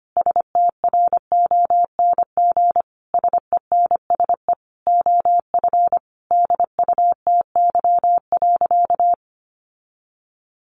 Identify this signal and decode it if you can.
Morse code, heard as a keyed tone, about 25 wpm, 700 Hz. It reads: STRONG SENSE OF DUTY.
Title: Sentences from Top 1500 Words 25wpm